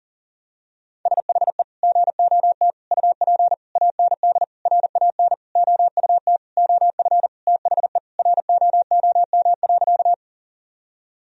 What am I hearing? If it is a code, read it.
Morse code, heard as a keyed tone, about 40 wpm, 700 Hz. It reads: SHE GOT UP AND RAN OUT OF THE ROOM.
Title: Sentences from Top 1000 Words - Review Part 3 - 40wpm